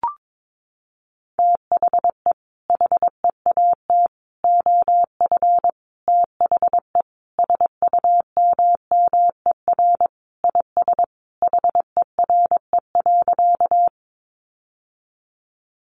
Morse code, heard as a keyed tone, about 22 wpm, 700 Hz. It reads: THE HEAT OF THE SUMMER IS HERE.
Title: Sentences from Top 1100 Words 22wpm